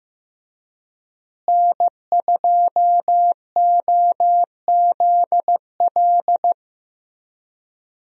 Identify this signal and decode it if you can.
Morse code, heard as a keyed tone, about 15 wpm, 700 Hz. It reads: N2OZL
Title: USA Call Signs - 5 Characters - Repeated 3x - 15wpm